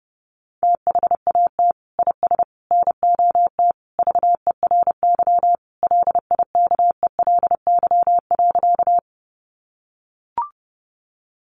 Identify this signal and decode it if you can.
Morse code, heard as a keyed tone, about 30 wpm, 700 Hz. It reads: THAT IS NOT VERY LIKELY.
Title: Sentences from Top 600 Words 30wpm